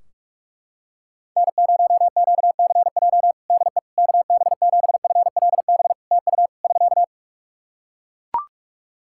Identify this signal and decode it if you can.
Morse code, heard as a keyed tone, about 44 wpm, 700 Hz. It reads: N0QCJ DE KB7FLB TU <SK>
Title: Practice QSO - Exchange 4 - Ending - 45wpm